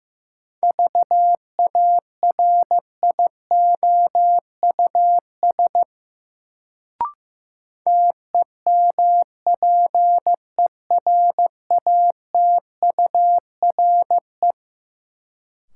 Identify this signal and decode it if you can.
Morse code, heard as a keyed tone, about 15 wpm, 700 Hz. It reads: VARIOUS TEMPERATURE
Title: Sentences from Top 1000 Words - Review Part 4 - New Word Drill - 15wpm